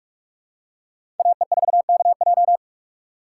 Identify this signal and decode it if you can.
Morse code, heard as a keyed tone, about 41 wpm, 700 Hz. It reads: AE4KJ
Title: USA Call Signs - 5 Characters - 3x Character Spacing - 45wpm